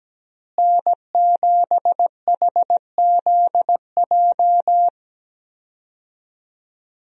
Morse code, heard as a keyed tone, about 17 wpm, 700 Hz. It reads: N7HZJ
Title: USA Call Signs - 5 Characters - Repeated 3x - 17wpm